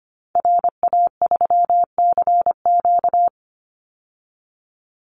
Morse code, heard as a keyed tone, about 25 wpm, 700 Hz. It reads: RA3CQ
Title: Call Signs Repeated 2x - 25wpm